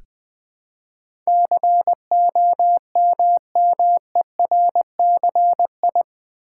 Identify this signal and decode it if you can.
Morse code, heard as a keyed tone, about 20 wpm, 700 Hz. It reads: COMMERCI
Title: Build A Word - 4 Letters At A Time - 10-Letter Words - 20wpm